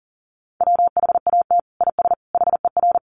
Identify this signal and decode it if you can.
Morse code, heard as a keyed tone, about 40 wpm, 700 Hz. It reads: WHAT IS HER
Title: Sentences from Top 1500 Words 40wpm